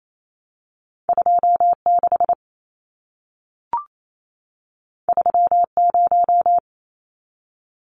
Morse code, heard as a keyed tone, about 28 wpm, 700 Hz. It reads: 26 30
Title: Numbers - 2-Digit 28wpm